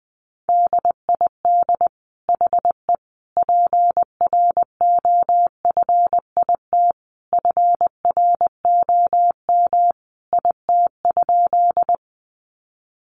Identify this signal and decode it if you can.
Morse code, heard as a keyed tone, about 20 wpm, 700 Hz. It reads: DID HE PROFIT FROM IT?
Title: Sentences from Top 800 Words 20wpm